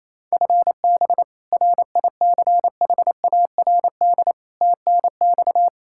Morse code, heard as a keyed tone, about 27 wpm, 700 Hz. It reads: FB RICHARD TNX
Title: Practice QSO - Exchange 3 - More Info - 28wpm